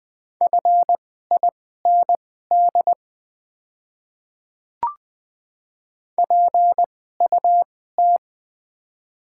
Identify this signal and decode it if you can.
Morse code, heard as a keyed tone, about 20 wpm, spaced as for 14 wpm, 700 Hz. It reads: FIND PUT
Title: 2x Character Spacing - Top 500 Words - 20wpm